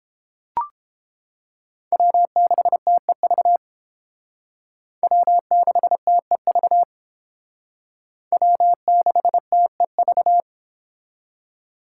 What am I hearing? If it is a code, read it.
Morse code, heard as a keyed tone, about 30 wpm, 700 Hz. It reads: W6TEV W6TEV W6TEV